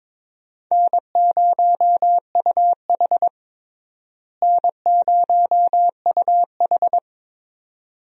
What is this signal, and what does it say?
Morse code, heard as a keyed tone, about 22 wpm, 700 Hz. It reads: N0UH N0UH
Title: USA Call Signs - 4 Characters - Repeated 2x - 22wpm